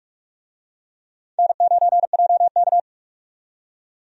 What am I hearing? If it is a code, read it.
Morse code, heard as a keyed tone, about 42 wpm, 700 Hz. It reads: N9JK
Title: USA Call Signs - 4 Characters - 45wpm